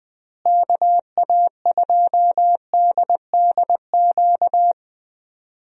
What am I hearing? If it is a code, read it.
Morse code, heard as a keyed tone, about 20 wpm, 700 Hz. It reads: KA2DDQ
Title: USA Call Signs - 6 Characters - Repeated 3x - 20wpm